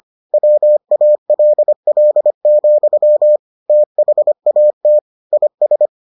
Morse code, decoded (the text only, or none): WALL, THAT IS